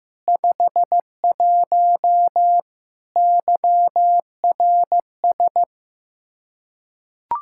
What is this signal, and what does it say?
Morse code, heard as a keyed tone, about 15 wpm, 700 Hz. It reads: 51 YRS
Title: Practice QSO - Exchange 3 - More Info - 15wpm